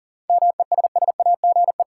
Morse code, heard as a keyed tone, about 40 wpm, 700 Hz. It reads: MESSAGE
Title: Sentences from Top 1100 Words - New Word Drill Alt - 40wpm